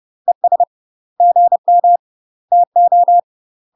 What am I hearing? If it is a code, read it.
Morse code, heard as a keyed tone, about 30 wpm, 700 Hz. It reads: ES GM TO
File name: UK MASTER QSOs TRAINING / 30WPM / UK MASTER QSOs 0013